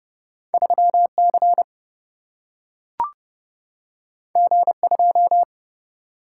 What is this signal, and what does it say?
Morse code, heard as a keyed tone, about 31 wpm, 700 Hz. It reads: 3 C G 2
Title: Single-Letter Single-Number 30wpm